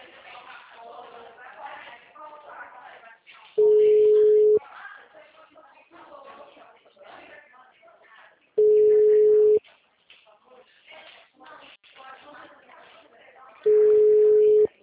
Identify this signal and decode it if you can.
Ringback tone (Europe, 425 Hz), with speech.
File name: out- -2010- - - .79246